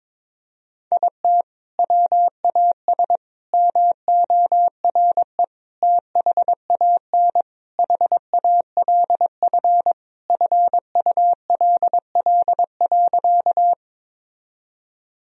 Morse code, heard as a keyed tone, about 22 wpm, 700 Hz. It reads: IT WAS MORE THAN HALF FULL.
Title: Sentences from Top 1000 Words - Review Part 3 - 22wpm